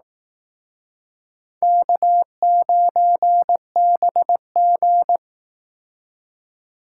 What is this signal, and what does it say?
Morse code, heard as a keyed tone, about 18 wpm, 700 Hz. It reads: K9BG
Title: Speed Racing - USA Call Signs 15wpm